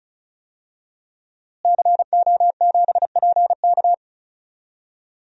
Morse code, heard as a keyed tone, about 34 wpm, 700 Hz. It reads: CO7PK
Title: DXCC-100 Call Signs - 5 Characters - Repeated 2x - 35wpm